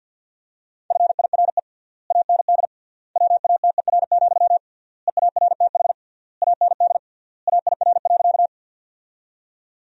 Morse code, heard as a keyed tone, about 50 wpm, 700 Hz. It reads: FIRE AND WATER, EARTH AND AIR.
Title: 3x Word-Spacing - Sentences from Top 400 Words - 50wpm